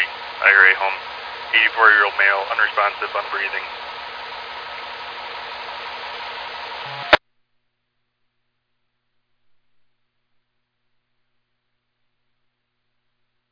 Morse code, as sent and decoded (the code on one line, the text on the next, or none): none